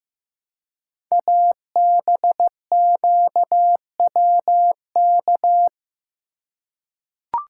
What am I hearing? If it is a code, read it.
Morse code, heard as a keyed tone, about 15 wpm, 700 Hz. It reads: ABQWK
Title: Code Groups - 5 Letters - 15wpm